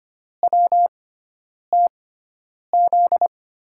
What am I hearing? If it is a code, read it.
Morse code, heard as a keyed tone, about 25 wpm, spaced as for 9 wpm, 700 Hz. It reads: WTZ
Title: USA Call Signs - 6 Characters - 6x Character Spacing - 25wpm